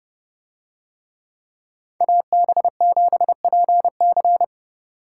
Morse code, heard as a keyed tone, about 29 wpm, 700 Hz. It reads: AB7PC